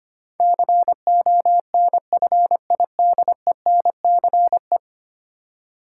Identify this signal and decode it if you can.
Morse code, heard as a keyed tone, about 25 wpm, 690 Hz. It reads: CONFIDENCE
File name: 10 Letter Words - First 3 Letters Given - 25wpm